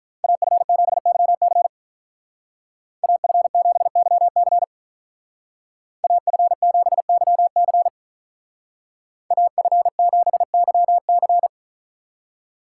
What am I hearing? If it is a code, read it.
Morse code, heard as a keyed tone, about 43 wpm, 700 Hz. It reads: AF7YC AF7YC AF7YC AF7YC
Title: Speed Racing - USA 2x2 Call Signs 35wpm